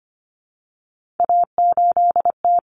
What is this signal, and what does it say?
Morse code, heard as a keyed tone, about 25 wpm, 700 Hz. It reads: A8T
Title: Common Cut Numbers - 3-Digit 25wpm